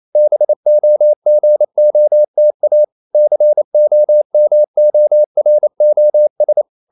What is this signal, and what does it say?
Morse code, heard as a keyed tone, about 28 wpm, 600 Hz. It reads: BOGOTA COMOROS
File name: CW / Practice / NationalCapitals28